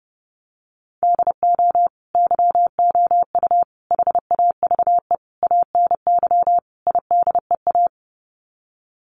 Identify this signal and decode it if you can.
Morse code, heard as a keyed tone, about 30 wpm, 700 Hz. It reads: DO YOU HAVE ANY IDEA
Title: Sets of 5 Words Encore 30wpm